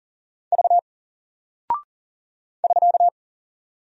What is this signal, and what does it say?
Morse code, heard as a keyed tone, about 40 wpm, 700 Hz. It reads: V <SK>
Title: Single Character (Letter:Number:Punctuation:Prosign) Alt 40wpm